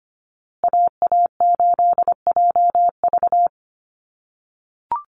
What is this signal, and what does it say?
Morse code, heard as a keyed tone, about 25 wpm, 700 Hz. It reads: AA8JV
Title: USA Call Signs Repeated 2x 25wpm